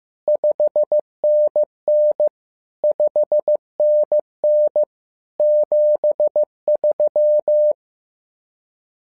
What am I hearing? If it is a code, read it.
Morse code, heard as a keyed tone, about 15 wpm, 600 Hz. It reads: 5NN 5NN 73
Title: SOTA QSO Practice 15wpm